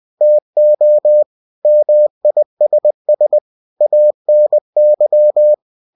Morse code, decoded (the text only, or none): TO MISS ANY